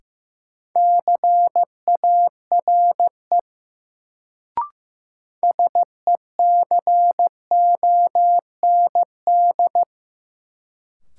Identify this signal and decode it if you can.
Morse code, heard as a keyed tone, about 15 wpm, 700 Hz. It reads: CARE SECOND
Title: Sentences from Top 300 Words - New Word Drill - 15wpm